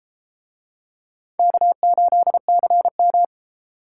Morse code, heard as a keyed tone, about 32 wpm, 700 Hz. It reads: K8CM